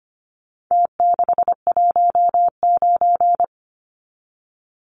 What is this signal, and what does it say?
Morse code, heard as a keyed tone, about 25 wpm, 700 Hz. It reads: T619
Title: Numbers - Contest Style - 4-Digit 25wpm